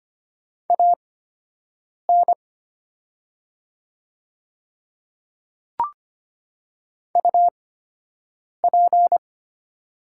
Morse code, heard as a keyed tone, about 25 wpm, spaced as for 7 wpm, 700 Hz. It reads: AN UP
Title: 8x Character Spacing - Top 100 Words - 25wpm